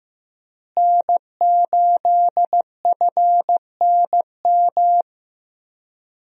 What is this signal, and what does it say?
Morse code, heard as a keyed tone, about 16 wpm, 700 Hz. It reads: N8FNM
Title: USA Call Signs - 5 Characters - Repeated 3x - 15wpm